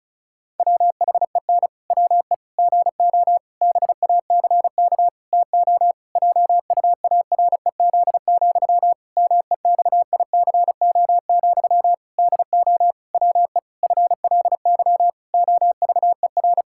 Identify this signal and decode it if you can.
Morse code, heard as a keyed tone, about 34 wpm, 700 Hz. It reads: WHEN WE GO BACK TO JUAREZ, MEXICO, DO WE FLY OVER
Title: Pangram Challenge v2 - 35wpm